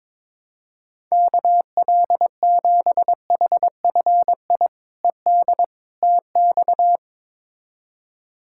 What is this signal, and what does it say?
Morse code, heard as a keyed tone, about 22 wpm, 700 Hz. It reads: KL7HFI ED TX
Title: K1USN SST QSO Practice - 22wpm